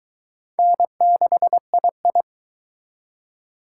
Morse code, heard as a keyed tone, about 22 wpm, 700 Hz. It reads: N6II